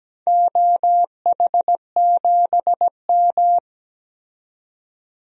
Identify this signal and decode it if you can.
Morse code, heard as a keyed tone, about 17 wpm, 700 Hz. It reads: OH7M
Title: DXCC-100 Call Signs - 4 Characters - 17wpm